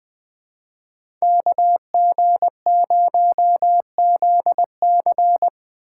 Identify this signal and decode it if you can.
Morse code, heard as a keyed tone, about 20 wpm, 700 Hz. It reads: KG0ZC
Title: USA Call Signs - 5 Characters - 4x Character Spacing - 20wpm